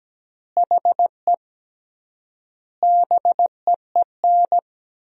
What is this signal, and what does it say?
Morse code, heard as a keyed tone, about 17 wpm, 700 Hz. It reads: HE BEEN
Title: 3x Word-Spacing - Sentences from Top 100 Words - 17wpm